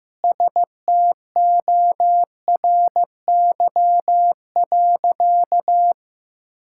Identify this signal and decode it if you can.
Morse code, heard as a keyed tone, about 15 wpm, 700 Hz. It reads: STORY.
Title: Sentences from Top 1000 Words - Review Part 5 - 15wpm